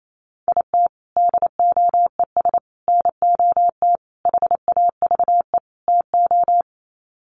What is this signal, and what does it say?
Morse code, heard as a keyed tone, about 28 wpm, 700 Hz. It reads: IT DOES NOT HAVE TO
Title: Sets of 5 Words 28wpm